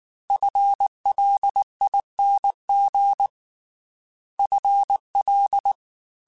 Morse code, heard as a keyed tone, about 19 wpm, 800 Hz. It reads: FLING FL
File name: FL / FL mp3s / FL Prefixes 3x2 Twice @18